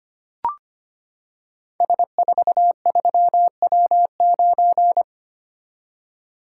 Morse code, heard as a keyed tone, about 25 wpm, 700 Hz. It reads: S43W9
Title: Code Groups - 5 Letter:Numbers - 25wpm